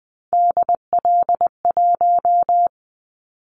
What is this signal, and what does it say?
Morse code, heard as a keyed tone, about 20 wpm, 700 Hz. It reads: DL1
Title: Callsign Prefixes for Top 50 DXCC - 20wpm